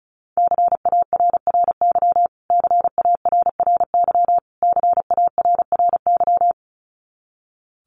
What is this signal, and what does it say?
Morse code, heard as a keyed tone, about 34 wpm, 700 Hz. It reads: CARRY CARRY CARRY
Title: Top 500 Words Repeated 35wpm